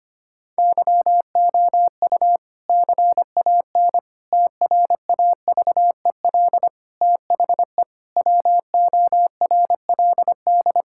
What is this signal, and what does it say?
Morse code, heard as a keyed tone, about 25 wpm, 700 Hz. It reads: YOU CAN TRAVEL THE WORLD
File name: practice / 25wpm / Difficult Sentences with Hints - 25wpm